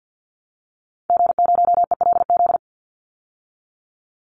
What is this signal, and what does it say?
Morse code, heard as a keyed tone, about 48 wpm, 700 Hz. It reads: G0ELZ
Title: Call Signs Repeated 3x - 50wpm